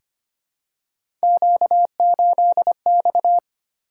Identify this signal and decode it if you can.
Morse code, heard as a keyed tone, about 25 wpm, 700 Hz. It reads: Q8X
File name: Three Letter-Number Combinations - 25wpm